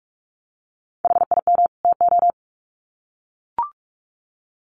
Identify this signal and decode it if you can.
Morse code, heard as a keyed tone, about 44 wpm, 700 Hz. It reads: HIM TO
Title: Sets of 2 Words 45wpm